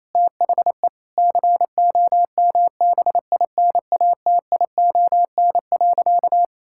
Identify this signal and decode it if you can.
Morse code, heard as a keyed tone, about 28 wpm, 700 Hz. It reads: THE COMBINATION.
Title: Sentences from Top 1000 Words - Review Part 6 - 28wpm